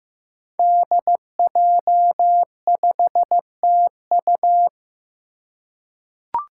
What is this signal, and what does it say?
Morse code, heard as a keyed tone, about 15 wpm, 700 Hz. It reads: DJ5TU
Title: Five Challenging Letter-Number Combinations 15wpm